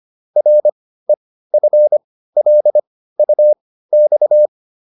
Morse code, heard as a keyed tone, about 25 wpm, spaced as for 15 wpm, 600 Hz. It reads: REFLUX